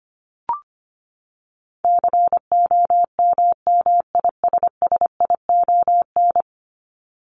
Morse code, heard as a keyed tone, about 25 wpm, 700 Hz. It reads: COMMISSION
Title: Top 1251-1500 Words 25wpm